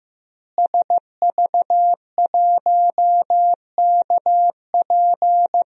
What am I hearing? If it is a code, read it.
Morse code, heard as a keyed tone, about 15 wpm, 700 Hz. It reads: SV1KP
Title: DXCC-100 Call Signs - 5 Characters - Repeated 2x - 15wpm